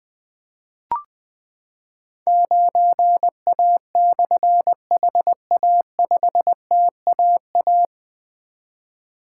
Morse code, heard as a keyed tone, about 20 wpm, 700 Hz. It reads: 9A/HA5TAA
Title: Difficult Call Signs 20wpm